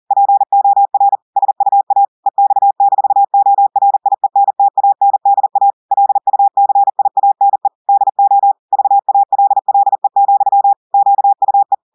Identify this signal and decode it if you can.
Morse code, heard as a keyed tone, about 40 wpm, 800 Hz. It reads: POR SUA EX-ORIENTANDA LUCIANE DO VALLE, QUE
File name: audios / portugais / portugais texte en clair 1 vitesse 40 mots minutes